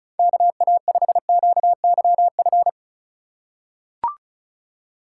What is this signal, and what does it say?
Morse code, heard as a keyed tone, about 36 wpm, 700 Hz. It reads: KA5QYF
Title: USA Call Signs - 6 Characters - 3x Character Spacing - 35wpm